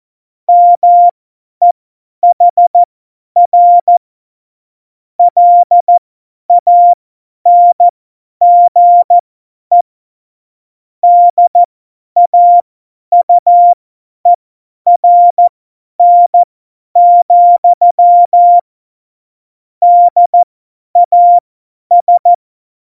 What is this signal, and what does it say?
Morse code, heard as a keyed tone, about 14 wpm, spaced as for 10 wpm, 700 Hz. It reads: MEHR LANGE DAUERN, DAS
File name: Morsekurs / Witze / Witz 26 Mai 2020, 50 F70